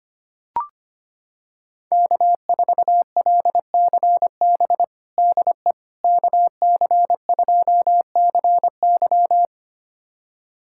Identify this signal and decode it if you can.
Morse code, heard as a keyed tone, about 25 wpm, 700 Hz. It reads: K4LCB DE KC2CY
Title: USA Callsign DE Callsign - 5 Characters - 25wpm